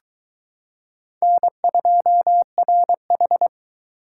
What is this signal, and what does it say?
Morse code, heard as a keyed tone, about 22 wpm, 700 Hz. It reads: N2RH